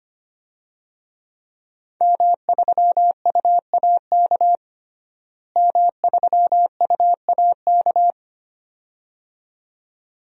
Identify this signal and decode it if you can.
Morse code, heard as a keyed tone, about 25 wpm, 700 Hz. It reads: M3UAK M3UAK
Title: DXCC-100 Call Signs - 5 Characters - Repeated 2x - 25wpm